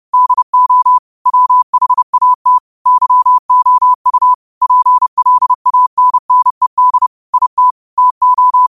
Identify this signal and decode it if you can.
Morse code, heard as a keyed tone, about 30 wpm, 1000 Hz. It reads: DO WHAT YOU PLANNED IT TO
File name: qotdfiles / 30wpm